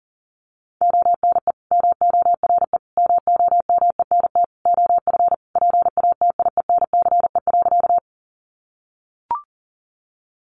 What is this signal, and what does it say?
Morse code, heard as a keyed tone, about 40 wpm, 700 Hz. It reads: ONE MORE MOMENT OF PATIENCE.
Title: Sentences from Top 1300 Words 40wpm